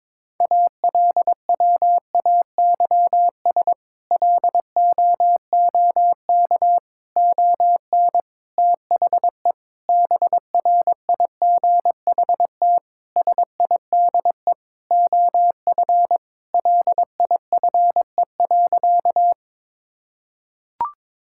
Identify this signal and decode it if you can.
Morse code, heard as a keyed tone, about 22 wpm, 700 Hz. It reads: ALWAYS LOOK ON THE BRIGHT SIDE OF LIFE.
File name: practice / 22wpm / Sentences with 8-Word Phrases - 22wpm